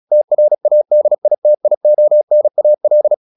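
Morse code, heard as a keyed tone, about 36 wpm, 600 Hz. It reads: TRADITIONAL
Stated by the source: Morse practice at 36 wpm, text of Ken Follett's novel Never, from CW med Gnister.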